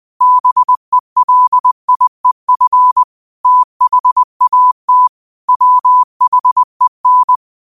Morse code, heard as a keyed tone, about 20 wpm, 1000 Hz. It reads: BELIEF THAT WHEN
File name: qotdfiles / 20wpm